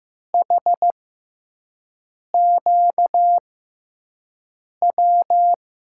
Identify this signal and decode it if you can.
Morse code, heard as a keyed tone, about 15 wpm, spaced as for 5 wpm, 700 Hz. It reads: HQW
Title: USA Call Signs - 6 Characters - 6x Character Spacing - 15wpm